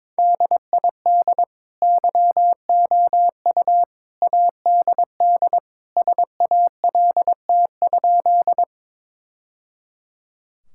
Morse code, heard as a keyed tone, about 21 wpm, 700 Hz. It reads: DID YOU ADD SALT?
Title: Sentences from Top 1000 Words - Review Part 5 - 22wpm